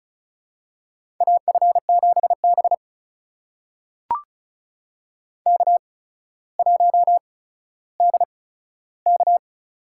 Morse code, heard as a keyed tone, about 36 wpm, spaced as for 15 wpm, 700 Hz. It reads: AF7B K1DK
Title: USA Call Signs - 4 Characters - 8x Character Spacing - 35wpm